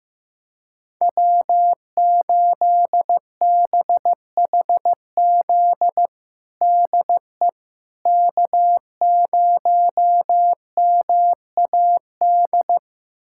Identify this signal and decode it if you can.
Morse code, heard as a keyed tone, about 15 wpm, 700 Hz. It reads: W8BHZ DE K0MAD
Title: USA Callsign DE Callsign - 5 Characters - 15wpm